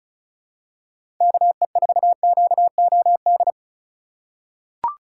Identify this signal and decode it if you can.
Morse code, heard as a keyed tone, about 36 wpm, 700 Hz. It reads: KE4QOD
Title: USA Call Signs - 6 Characters - 3x Character Spacing - 35wpm